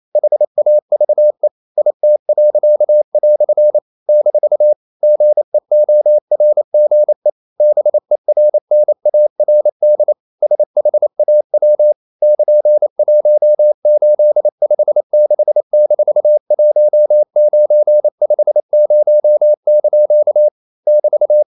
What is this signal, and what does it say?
Morse code, heard as a keyed tone, about 28 wpm, 600 Hz. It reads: HAVE IT." - GEORGE BERNARD SHAW <KN>1856-1950) <BT>
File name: CW / Practice / EnglishQuotations0128